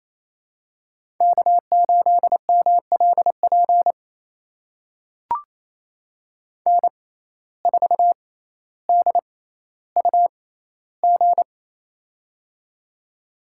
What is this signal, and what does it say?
Morse code, heard as a keyed tone, about 28 wpm, spaced as for 14 wpm, 700 Hz. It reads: K8MLP N4DUG
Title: USA Call Signs - 5 Characters - 6x Character Spacing - 28wpm